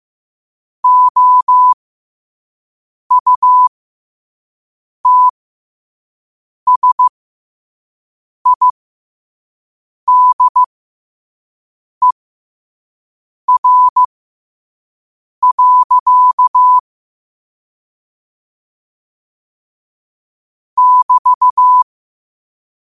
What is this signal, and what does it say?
Morse code, heard as a keyed tone, about 15 wpm, spaced as for 5 wpm, 1000 Hz. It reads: OUTSIDER. <BT>
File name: qotdfiles / 5wpm